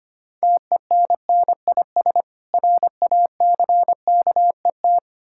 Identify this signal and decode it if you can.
Morse code, heard as a keyed tone, about 25 wpm, 700 Hz. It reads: TENNIS RACKET
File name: practice / 25wpm / Difficult Words with Hints - 25wpm